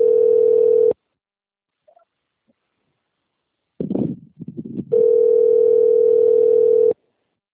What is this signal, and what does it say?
Ringback tone (North America), 440 + 480 Hz.